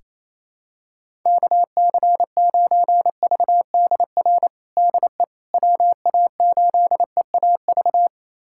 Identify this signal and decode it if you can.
Morse code, heard as a keyed tone, about 28 wpm, 700 Hz. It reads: KC9VDR DE WA8EAV
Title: USA Callsign DE Callsign 28wpm